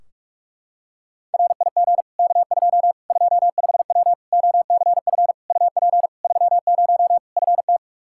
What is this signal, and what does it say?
Morse code, heard as a keyed tone, about 46 wpm, 700 Hz. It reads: RIG K2 25W OCF UP 30 FT
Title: Practice QSO - Exchange 2 - Info - 45wpm